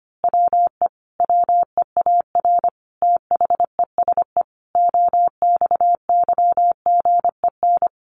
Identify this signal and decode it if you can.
Morse code, heard as a keyed tone, about 25 wpm, 700 Hz. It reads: WE WEAR THESE OXYGEN
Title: A Sound Of Thunder 25wpm